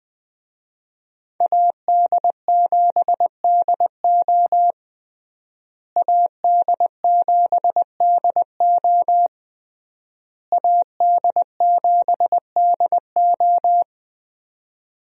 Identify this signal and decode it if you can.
Morse code, heard as a keyed tone, about 20 wpm, 700 Hz. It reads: AD7DO AD7DO AD7DO
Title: USA Call Signs - 5 Characters - Repeated 3x - 20wpm